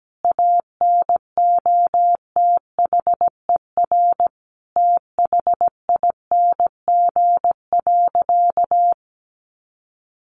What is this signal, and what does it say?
Morse code, heard as a keyed tone, about 17 wpm, 700 Hz. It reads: ANOTHER THING.
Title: Sentences from Top 700 Words 17wpm